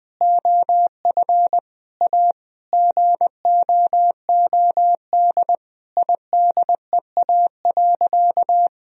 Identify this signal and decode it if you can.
Morse code, heard as a keyed tone, about 21 wpm, 700 Hz. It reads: OF A GOOD IDEA.
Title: Sentences from Top 1000 Words - Review Part 4 - 20wpm